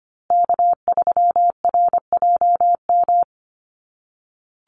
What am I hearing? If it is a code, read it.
Morse code, heard as a keyed tone, about 25 wpm, 700 Hz. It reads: K3RJM
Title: USA Call Signs Repeated 2x 25wpm